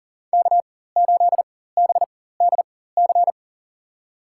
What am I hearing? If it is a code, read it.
Morse code, heard as a keyed tone, about 40 wpm, spaced as for 19 wpm, 700 Hz. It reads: K8BDC
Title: USA Call Signs - 5 Characters - 4x Character Spacing - 40wpm